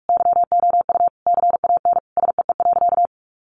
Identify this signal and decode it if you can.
Morse code, heard as a keyed tone, about 44 wpm, 700 Hz. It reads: YOU CAN SEE.
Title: Sentences from Top 1000 Words 45wpm